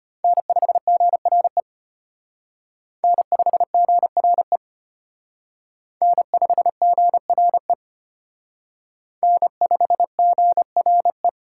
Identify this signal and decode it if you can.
Morse code, heard as a keyed tone, about 31 wpm, 700 Hz. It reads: N5GRE N5GRE N5GRE N5GRE
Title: Speed Racing - USA Call Signs 25wpm